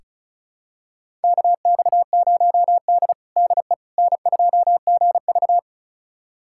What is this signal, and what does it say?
Morse code, heard as a keyed tone, about 36 wpm, 700 Hz. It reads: KX0D DE N2GV
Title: USA Callsign DE Callsign - 4 Characters - 35wpm